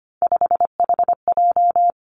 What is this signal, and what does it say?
Morse code, heard as a keyed tone, about 25 wpm, 700 Hz. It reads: 5HJ
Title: Three Challenging Letter-Number Combinations 25wpm